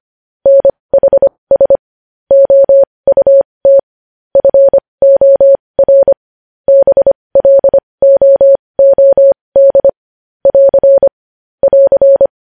Morse code, heard as a keyed tone, about 25 wpm, spaced as for 20 wpm, 550 Hz. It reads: NHS OUT FOR BLOOD <AR> <AR>
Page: News Headlines in Morse Code at 20 WPM